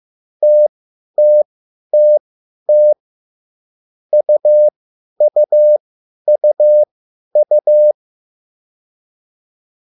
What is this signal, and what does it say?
Morse code, heard as a keyed tone, about 15 wpm, spaced as for 10 wpm, 600 Hz. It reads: TTTT UUUU